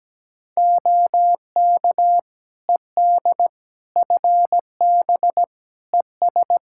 Morse code, heard as a keyed tone, about 17 wpm, 700 Hz. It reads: OK ED FB ES
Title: K1USN SST QSO Practice 17wpm